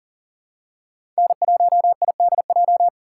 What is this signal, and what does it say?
Morse code, heard as a keyed tone, about 40 wpm, 700 Hz. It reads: N1IDJ
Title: USA Call Signs - 5 Characters - 3x Character Spacing - 40wpm